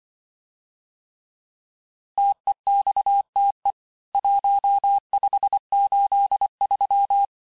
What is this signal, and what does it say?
Morse code, heard as a keyed tone, about 24 wpm, 800 Hz. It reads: TEXTE 1583
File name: cours-du-lundi files / cours-1583--texte-clair-18-puis-23-mpm